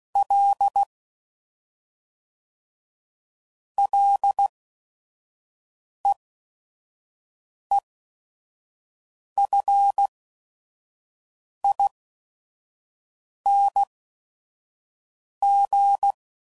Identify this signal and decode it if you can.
Morse code, heard as a keyed tone, about 16 wpm, spaced as for 5 wpm, 800 Hz. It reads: L LEEFING
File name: Prefixes / FLs & THs / Single Fs Ls